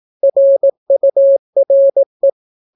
Morse code, heard as a keyed tone, about 18 wpm, 550 Hz. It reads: RURE